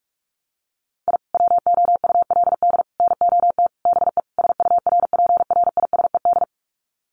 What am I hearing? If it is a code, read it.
Morse code, heard as a keyed tone, about 44 wpm, 700 Hz. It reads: I WOULD NOT BE SURPRISED
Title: Sets of 5 Words Encore2 45wpm